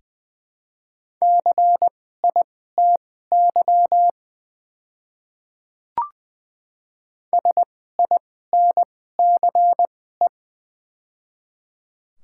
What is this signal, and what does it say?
Morse code, heard as a keyed tone, about 20 wpm, spaced as for 14 wpm, 690 Hz. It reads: CITY SINCE
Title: 2x Character Spacing - Top 300 Words - 20wpm